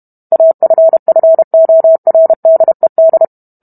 Morse code, heard as a keyed tone, about 32 wpm, 650 Hz. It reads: AFFORDED